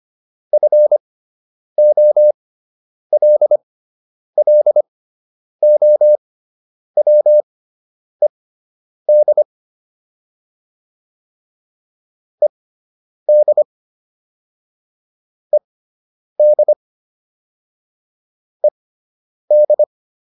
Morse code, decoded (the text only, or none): FOLLOWED ED ED ED